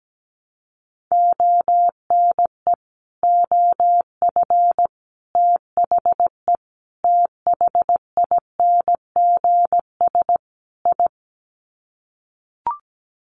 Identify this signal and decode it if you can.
Morse code, heard as a keyed tone, about 17 wpm, 700 Hz. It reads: ONE OF THE THINGS I
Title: Sets of 5 Words 17wpm